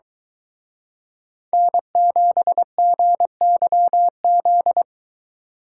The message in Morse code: N7GYZ

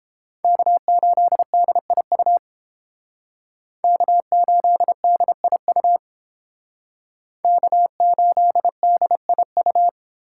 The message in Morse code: K8DIU K8DIU K8DIU

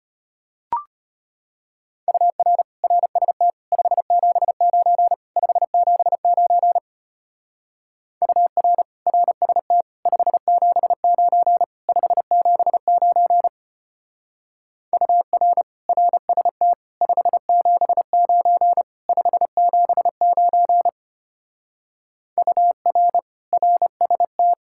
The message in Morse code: UR RST 579 579 UR RST 579 579 UR RST 579 579 UR RST

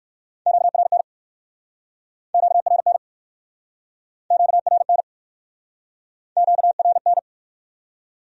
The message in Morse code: QRN QRN QRN QRN

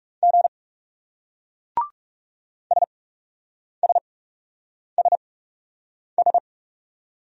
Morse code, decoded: G SSSS

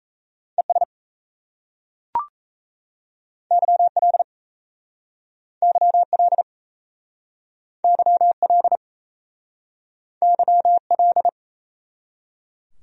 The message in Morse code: ES YL YL YL YL

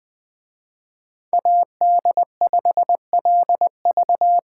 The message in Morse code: AD5LV

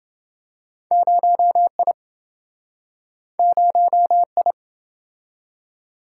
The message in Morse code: 0I 0I

